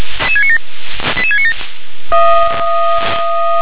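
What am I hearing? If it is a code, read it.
Five-tone selective call (ZVEI), heard as quick runs of five tones, tones of about 70 ms. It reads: 88697
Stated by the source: Five-tone ZVEI selective call received by SDR.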